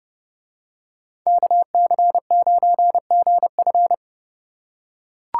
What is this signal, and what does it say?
Morse code, heard as a keyed tone, about 29 wpm, 700 Hz. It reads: KC9GF